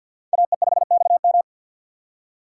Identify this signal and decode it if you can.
Morse code, heard as a keyed tone, about 50 wpm, 700 Hz. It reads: AE5XM
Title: USA Call Signs - 5 Characters - 4x Character Spacing - 50wpm